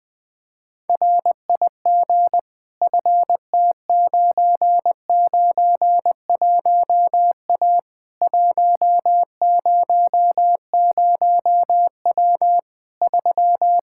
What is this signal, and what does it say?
Morse code, heard as a keyed tone, about 20 wpm, 700 Hz. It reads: RIG FT991A 100W 3
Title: K1USN SST QSO Practice 20wpm